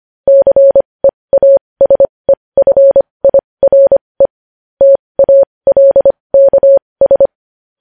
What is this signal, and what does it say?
Morse code, heard as a keyed tone, about 25 wpm, spaced as for 20 wpm, 550 Hz. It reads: CEASEFIRE TALKS
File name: podcast / News headlines-1x-25x20